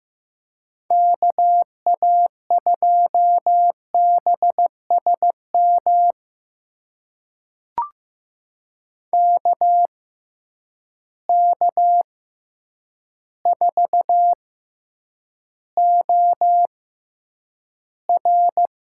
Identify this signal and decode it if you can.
Morse code, heard as a keyed tone, about 15 wpm, spaced as for 8 wpm, 700 Hz. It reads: KA2BSM KK4OR
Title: USA Call Signs - 6 Characters - 6x Character Spacing - 15wpm